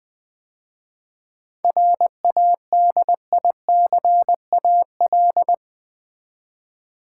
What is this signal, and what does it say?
Morse code, heard as a keyed tone, about 20 wpm, 700 Hz. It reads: RADICAL
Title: 7 Letter Words - First 3 Letters Given - 20wpm